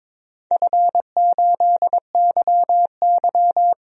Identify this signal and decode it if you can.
Morse code, heard as a keyed tone, about 22 wpm, 700 Hz. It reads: F8YY
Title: DXCC-100 Call Signs - 4 Characters - 22wpm